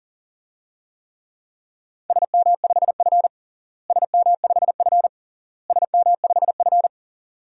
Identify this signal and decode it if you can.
Morse code, heard as a keyed tone, about 37 wpm, 700 Hz. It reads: SM5F SM5F SM5F
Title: DXCC-100 Call Signs - 4 Characters - Repeated 3x - 40wpm